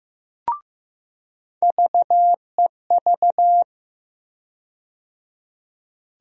Morse code, heard as a keyed tone, about 16 wpm, 700 Hz. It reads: VEV
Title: USA Call Sign Postfixes - 3 Letters 15wpm